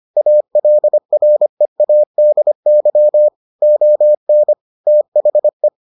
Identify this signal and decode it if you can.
Morse code, heard as a keyed tone, about 25 wpm, 600 Hz. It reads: ALREADY ON THE